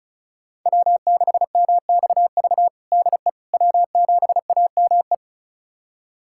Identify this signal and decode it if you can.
Morse code, heard as a keyed tone, about 36 wpm, 700 Hz. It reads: W6MXV DE W7AME
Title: USA Callsign DE Callsign - 5 Characters - 35wpm